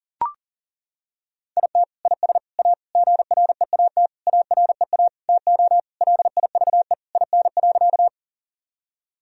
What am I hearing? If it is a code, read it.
Morse code, heard as a keyed tone, about 40 wpm, 700 Hz. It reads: IT IS A GREAT AREA TO LIVE IN.
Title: Sentences from Top 1000 Words - Review Part 2 - 40wpm